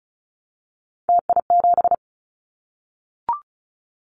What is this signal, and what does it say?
Morse code, heard as a keyed tone, about 36 wpm, 700 Hz. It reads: TI7
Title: Callsign Prefixes for Top 100 DXCC - 35wpm